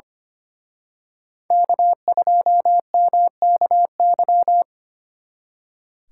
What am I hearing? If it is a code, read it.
Morse code, heard as a keyed tone, about 30 wpm, 700 Hz. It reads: K2MKY
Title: Speed Racing - USA 1x3 Call Signs 25wpm